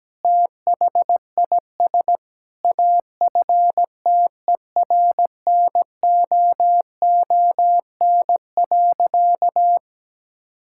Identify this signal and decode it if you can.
Morse code, heard as a keyed tone, about 17 wpm, 700 Hz. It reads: THIS AFTERNOON.
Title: Sentences from Top 1600 Words 17wpm